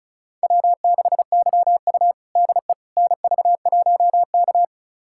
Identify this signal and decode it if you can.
Morse code, heard as a keyed tone, about 36 wpm, 700 Hz. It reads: W6YU DE NV1K
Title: USA Callsign DE Callsign - 4 Characters - 35wpm